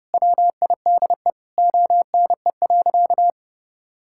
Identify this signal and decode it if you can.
Morse code, heard as a keyed tone, about 30 wpm, 700 Hz. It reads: WIDE ONE.
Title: Sentences from Top 1000 Words - Review Part 6 - 30wpm